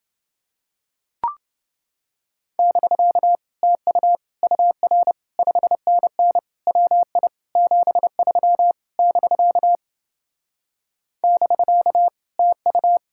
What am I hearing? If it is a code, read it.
Morse code, heard as a keyed tone, about 28 wpm, 700 Hz. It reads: <BK> TU UR 5NN WI 73 <BK> <BK> TU